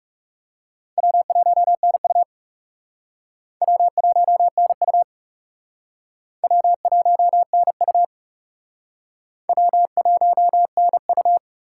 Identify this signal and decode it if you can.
Morse code, heard as a keyed tone, about 38 wpm, 700 Hz. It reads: W1NU W1NU W1NU W1NU